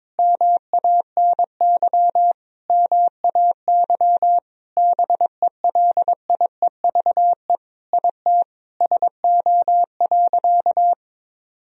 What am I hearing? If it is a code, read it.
Morse code, heard as a keyed tone, about 22 wpm, 700 Hz. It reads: MANY MAY BELIEVE IT SO.
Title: Sentences from Top 1000 Words - Review Part 4 - 22wpm